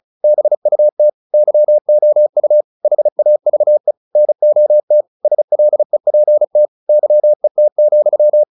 Morse code, heard as a keyed tone, about 35 wpm, 600 Hz. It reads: BUT YOU HAVE NOT SLEPT YET,
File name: the old man and the sea 2 35wpm